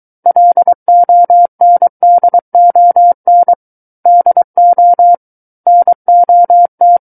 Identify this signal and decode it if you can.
Morse code, heard as a keyed tone, about 23 wpm, 700 Hz. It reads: LONDON DO NOT
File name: Book One / 14-Chapter 14 - 23 WPM